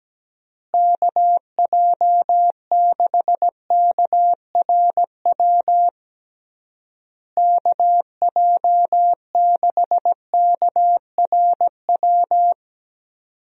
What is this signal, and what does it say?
Morse code, heard as a keyed tone, about 17 wpm, 700 Hz. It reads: KJ6KRW KJ6KRW
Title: USA Call Signs - 6 Characters - Repeated 3x - 17wpm